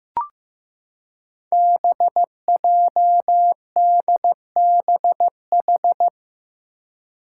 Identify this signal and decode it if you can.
Morse code, heard as a keyed tone, about 15 wpm, 700 Hz. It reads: BJDBH
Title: Five Challenging Letter-Number Combinations 15wpm